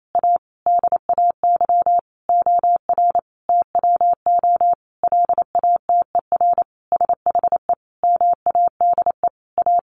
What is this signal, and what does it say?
Morse code, heard as a keyed tone, about 28 wpm, 700 Hz. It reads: A DAY OR TWO LATER SHE MADE A
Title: Sentences from Top 1100 Words 28wpm